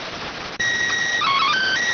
SSTV (slow-scan television), PD 290.